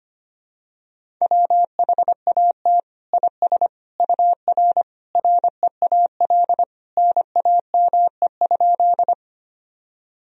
Morse code, heard as a keyed tone, about 25 wpm, 700 Hz. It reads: WHAT IS UR REAL NAME?
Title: Sentences from Top 300 Words and Abbreviations 25wpm